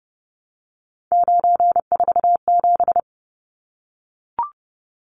Morse code, heard as a keyed tone, about 30 wpm, 700 Hz. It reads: 947
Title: Numbers - 3-Digit 30wpm